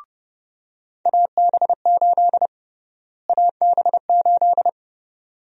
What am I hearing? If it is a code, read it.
Morse code, heard as a keyed tone, about 32 wpm, 700 Hz. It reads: AB8 AB8
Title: USA Call Sign Prefixes - Repeated 2x - 30wpm